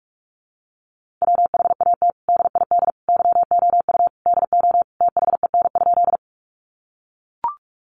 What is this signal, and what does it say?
Morse code, heard as a keyed tone, about 44 wpm, 700 Hz. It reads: WHAT DID YOU DO THEN?
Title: Sentences from Top 100 Words 45wpm